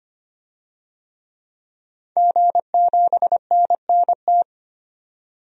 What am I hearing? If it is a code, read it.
Morse code, heard as a keyed tone, about 25 wpm, 700 Hz. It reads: G7NNT
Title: DXCC-100 Call Signs - 5 Characters - 25wpm